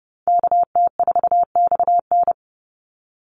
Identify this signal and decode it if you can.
Morse code, heard as a keyed tone, about 30 wpm, 700 Hz. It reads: KT4XN
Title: Call Signs Repeated 3x - 30wpm